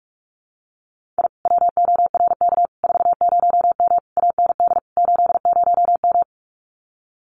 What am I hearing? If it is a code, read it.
Morse code, heard as a keyed tone, about 46 wpm, 700 Hz. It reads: I WORK 40M AND 80M
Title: Common QSO Phrases Encore 45wpm